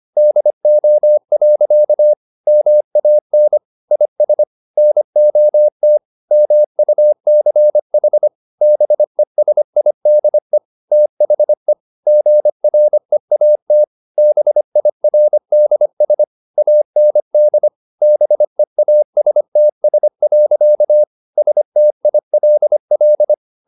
Morse code, heard as a keyed tone, about 25 wpm, 600 Hz. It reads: DO. MAN IS NOT MUCH BESIDE THE GREAT BIRDS AND BEASTS. STILL
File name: the old man and the sea part 1 25wpm